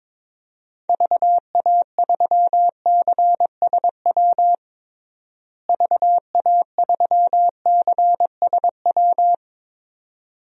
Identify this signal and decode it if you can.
Morse code, heard as a keyed tone, about 22 wpm, 700 Hz. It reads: VA3CSW VA3CSW
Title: DXCC-100 Call Signs - 6 Characters - Repeated 2x - 22wpm